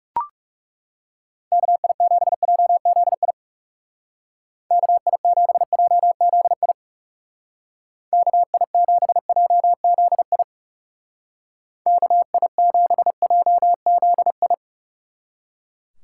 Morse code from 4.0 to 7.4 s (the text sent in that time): KI7JZI